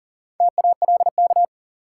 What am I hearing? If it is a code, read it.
Morse code, heard as a keyed tone, about 40 wpm, 700 Hz. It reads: TALK
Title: Sentences from Top 400 Words - New Word Drill - 40wpm